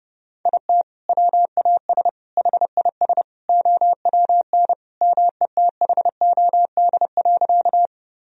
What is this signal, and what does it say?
Morse code, heard as a keyed tone, about 31 wpm, 700 Hz. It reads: IT WAS HIS OWN METHOD.
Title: Sentences from Top 1000 Words - Review Part 3 - 30wpm